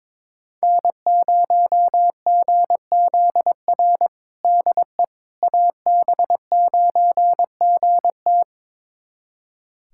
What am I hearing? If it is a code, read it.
Morse code, heard as a keyed tone, about 22 wpm, 700 Hz. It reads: N0GZR DE AB9GT
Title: USA Callsign DE Callsign - 5 Characters - 22wpm